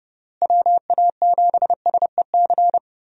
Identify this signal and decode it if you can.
Morse code, heard as a keyed tone, about 29 wpm, 700 Hz. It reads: WA7SEC
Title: USA Call Signs - 6 Characters - Repeated 3x - 30wpm